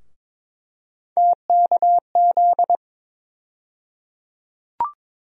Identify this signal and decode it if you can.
Morse code, heard as a keyed tone, about 22 wpm, 700 Hz. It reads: TKZ
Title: USA Call Sign Postfixes - 3 Letters 22wpm